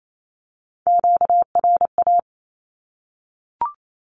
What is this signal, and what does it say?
Morse code, heard as a keyed tone, about 28 wpm, 700 Hz. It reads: QRA
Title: Q-Codes 28wpm